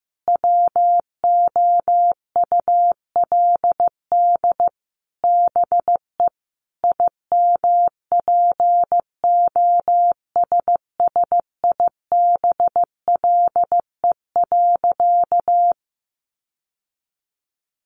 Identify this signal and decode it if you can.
Morse code, heard as a keyed tone, about 15 wpm, 700 Hz. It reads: WOULD BE IMPOSSIBLE.
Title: Sentences from Top 1100 Words 15wpm